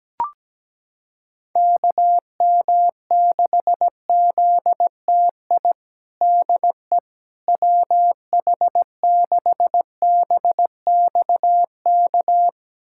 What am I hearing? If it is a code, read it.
Morse code, heard as a keyed tone, about 17 wpm, 700 Hz. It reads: KM6ZTI DE WH6BXK
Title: USA Callsign DE Callsign - 6 Characters - 17wpm